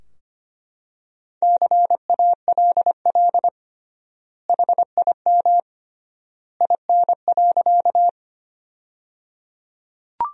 Morse code, heard as a keyed tone, about 25 wpm, 700 Hz. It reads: CALL HIM IN.
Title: 3x Word-Spacing - Sentences from Top 100 Words - 25wpm